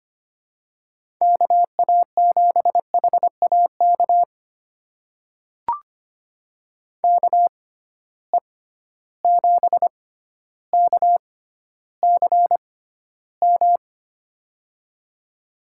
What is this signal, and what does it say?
Morse code, heard as a keyed tone, about 25 wpm, spaced as for 13 wpm, 700 Hz. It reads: KA7HAK KE7KCM
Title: USA Call Signs - 6 Characters - 6x Character Spacing - 25wpm